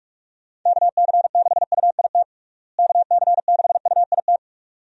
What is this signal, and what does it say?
Morse code, heard as a keyed tone, about 46 wpm, 700 Hz. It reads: KC6UIT KC6UIT
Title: USA Call Signs - 6 Characters - Repeated 2x - 45wpm